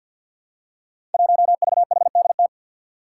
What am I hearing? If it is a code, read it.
Morse code, heard as a keyed tone, about 47 wpm, 700 Hz. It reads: 1VHDT
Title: Five Challenging Letter-Number Combinations 50wpm